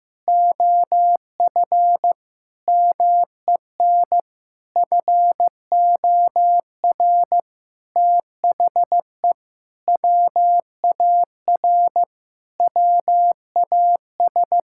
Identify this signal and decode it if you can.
Morse code, heard as a keyed tone, about 15 wpm, 700 Hz. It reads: OF MEN FOR THE WAR WAS
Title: Sentences from Top 1000 Words - Review Part 6 - 15wpm